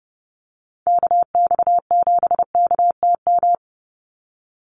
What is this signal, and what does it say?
Morse code, heard as a keyed tone, about 30 wpm, 700 Hz. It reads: KX7KTM
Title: USA Call Signs 30wpm